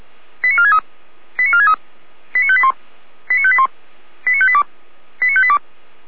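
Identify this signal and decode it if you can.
Five-tone selective call (ZVEI), heard as quick runs of five tones, tones of about 70 ms. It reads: 89463 89463 89581 89581 89582 89582
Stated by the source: ZVEI five-tone selective call 89380 received with an SDR.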